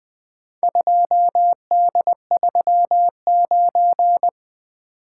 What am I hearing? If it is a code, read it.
Morse code, heard as a keyed tone, about 20 wpm, 700 Hz. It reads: 2D39
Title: Four Letter-Number Combinations - 20wpm